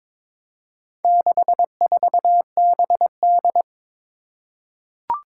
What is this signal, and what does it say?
Morse code, heard as a keyed tone, about 22 wpm, 700 Hz. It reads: 64BD
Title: Four Challenging Letter-Number Combinations 22wpm